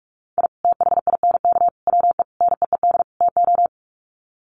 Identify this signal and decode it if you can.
Morse code, heard as a keyed tone, about 44 wpm, 700 Hz. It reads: I THINK WE NEED TO
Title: Sets of 5 Words 45wpm